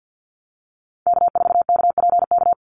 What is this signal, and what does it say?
Morse code, heard as a keyed tone, about 49 wpm, 700 Hz. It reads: K4KPK
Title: NA SOTA Activator Callsigns 50wpm